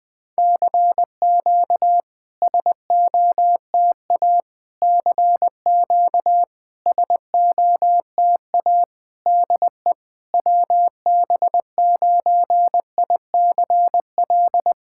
Morse code, heard as a keyed tone, about 20 wpm, 700 Hz. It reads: CQ SOTA CQ SOTA DE WB9ICL
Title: SOTA QSO Practice 20wpm